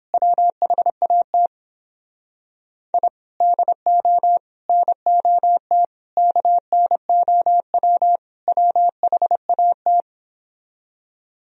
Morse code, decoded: WHAT I DO NOT KNOW WHAT

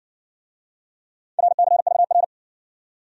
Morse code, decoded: RCVR

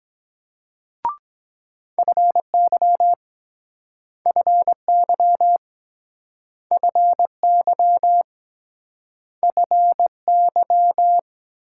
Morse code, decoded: FY FY FY FY

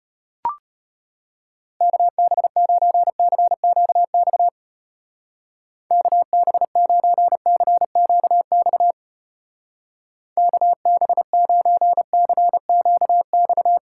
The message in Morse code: KB9CQX KB9CQX KB9CQX

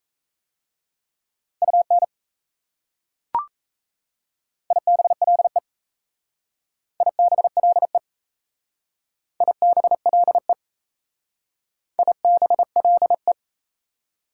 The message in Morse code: UN IBLE IBLE IBLE IBLE

